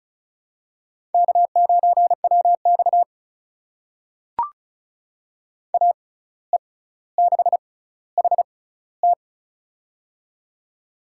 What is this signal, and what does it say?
Morse code, heard as a keyed tone, about 34 wpm, spaced as for 17 wpm, 700 Hz. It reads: K9WX AE6HT
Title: USA Call Signs - 6x Character Spacing - 35wpm